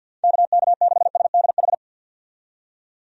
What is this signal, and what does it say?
Morse code, heard as a keyed tone, about 51 wpm, 700 Hz. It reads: KK6SDH
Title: USA Call Signs - 6 Characters - 50wpm